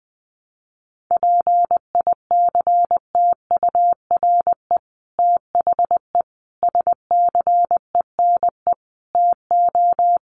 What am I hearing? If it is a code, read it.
Morse code, heard as a keyed tone, about 20 wpm, 700 Hz. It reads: PICTURE THE SCENE TO
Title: Sentences from Top 1100 Words 20wpm